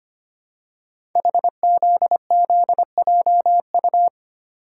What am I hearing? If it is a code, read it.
Morse code, heard as a keyed tone, about 25 wpm, 700 Hz. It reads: HZZJU
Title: Code Groups - 5 Letters - 25wpm